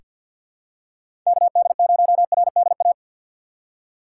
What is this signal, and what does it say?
Morse code, heard as a keyed tone, about 47 wpm, 700 Hz. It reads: KD0RDA
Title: USA Call Signs - 4x Character Spacing - 50wpm